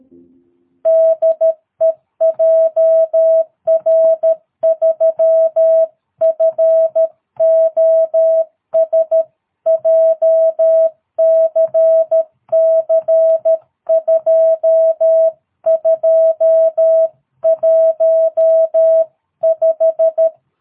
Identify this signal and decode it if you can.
Morse code, heard as a keyed tone, about 13 wpm, 650 Hz. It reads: DE JR3FOS JCC 2215
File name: jh8cbh / 2022cw